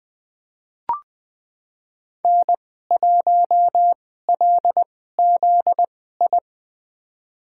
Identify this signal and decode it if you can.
Morse code, heard as a keyed tone, about 20 wpm, spaced as for 14 wpm, 700 Hz. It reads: N1LZI
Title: USA Call Signs - 5 Characters - 2x Character Spacing - 20wpm